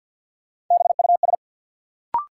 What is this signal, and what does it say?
Morse code, heard as a keyed tone, about 53 wpm, 700 Hz. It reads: BUS
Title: Sentences from Top 1000 Words - Review Part 6 - New Word Drill - 50wpm